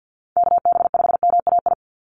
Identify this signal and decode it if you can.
Morse code, heard as a keyed tone, about 50 wpm, 700 Hz. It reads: KB5MAI
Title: USA Call Signs 50wpm